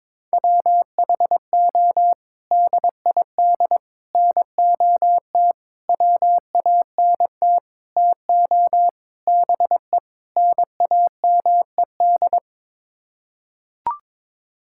Morse code, heard as a keyed tone, about 22 wpm, 700 Hz. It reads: WHO DID NOT WANT TO BE NAMED
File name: practice / 22wpm / Sets of 7 Words - 22wpm